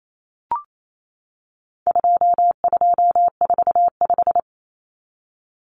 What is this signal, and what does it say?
Morse code, heard as a keyed tone, about 28 wpm, 700 Hz. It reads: 2245
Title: Numbers - 4-Digit 28wpm